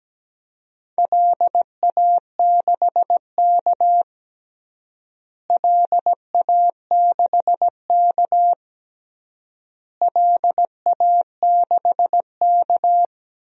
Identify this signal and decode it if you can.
Morse code, heard as a keyed tone, about 17 wpm, 700 Hz. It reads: LA6K LA6K LA6K
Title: DXCC-100 Call Signs - 4 Characters - Repeated 3x - 17wpm